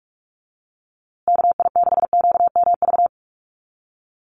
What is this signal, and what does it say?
Morse code, heard as a keyed tone, about 44 wpm, 700 Hz. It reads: KI6QMV
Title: USA Call Signs Repeated 3x 45wpm